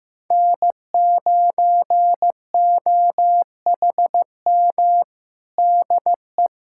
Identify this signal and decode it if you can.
Morse code, heard as a keyed tone, about 16 wpm, 700 Hz. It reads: N9OHM DE
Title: USA Callsign DE Callsign 15wpm